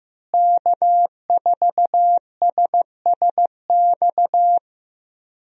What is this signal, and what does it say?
Morse code, heard as a keyed tone, about 15 wpm, 700 Hz. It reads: K4SSX
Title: USA Call Signs - 5 Characters - Repeated 3x - 15wpm